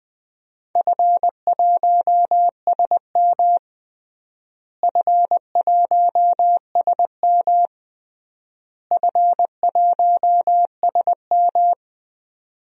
Morse code, heard as a keyed tone, about 20 wpm, 700 Hz. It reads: F1SM F1SM F1SM
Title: DXCC-100 Call Signs - 4 Characters - Repeated 3x - 20wpm